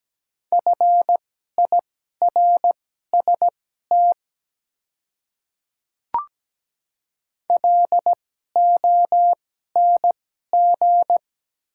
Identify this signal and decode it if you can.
Morse code, heard as a keyed tone, about 17 wpm, spaced as for 12 wpm, 700 Hz. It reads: FIRST LONG
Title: 2x Character Spacing - Top 100 Words - 17wpm